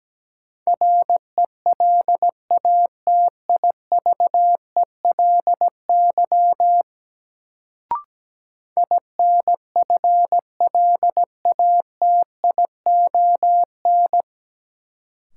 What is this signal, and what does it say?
Morse code, heard as a keyed tone, about 17 wpm, 700 Hz. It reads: RELATIVELY INFLATION
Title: Sentences from Top 1100 Words - New Word Drill Alt - 17wpm